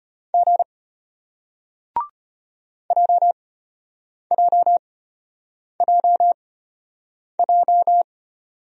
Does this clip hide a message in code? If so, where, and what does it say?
Yes: Morse code from 0.3 to 8.0 s, heard as a keyed tone, about 34 wpm, spaced as for 8 wpm, 700 Hz. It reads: G JJJJ